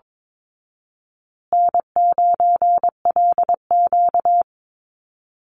Morse code, heard as a keyed tone, about 22 wpm, 700 Hz. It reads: N9LQ
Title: Call Signs Repeated 2x - 22wpm